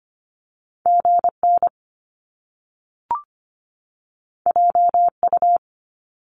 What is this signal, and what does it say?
Morse code, heard as a keyed tone, about 25 wpm, 700 Hz. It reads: GN JU
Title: Top Two Letter Combinations in Words 25wpm